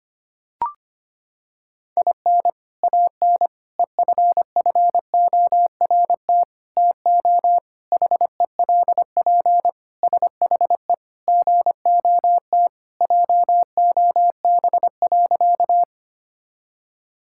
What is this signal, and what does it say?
Morse code, heard as a keyed tone, about 25 wpm, 700 Hz. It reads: IN AN EFFORT TO HELP SHE GOT JOB.
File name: practice / 25wpm / Sentences with 4-Word Phrases - 25wpm